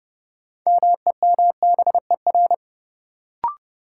Morse code, heard as a keyed tone, about 30 wpm, 700 Hz. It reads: MEMBER
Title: Sets of 3 Words - New Word Drill - 30wpm